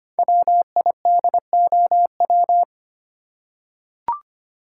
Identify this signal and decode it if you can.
Morse code, heard as a keyed tone, about 25 wpm, 700 Hz. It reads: WIDOW
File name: 5 Letter Words - First 3 Letters Given - 25wpm